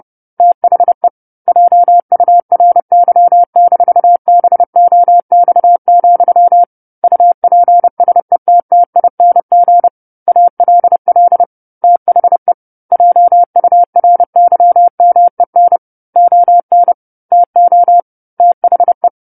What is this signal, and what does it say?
Morse code, heard as a keyed tone, about 30 wpm, 700 Hz. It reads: THE JURY-BOX, UPSETTING ALL THE JURYMEN ON TO THE
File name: 10-Chapter-10 30 WPM